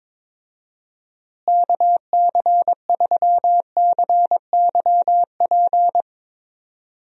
Morse code, heard as a keyed tone, about 23 wpm, 700 Hz. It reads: KC3CYP
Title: USA Call Signs - 6 Characters - Repeated 3x - 22wpm